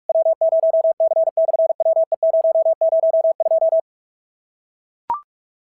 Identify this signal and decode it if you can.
Morse code, heard as a keyed tone, about 44 wpm, 650 Hz. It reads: W0C/WE002
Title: SOTA QSO Practice 45wpm